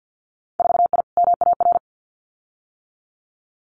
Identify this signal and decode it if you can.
Morse code, heard as a keyed tone, about 50 wpm, 700 Hz. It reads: 4I MAR
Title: Winter Field Day Exchanges Encore 50wpm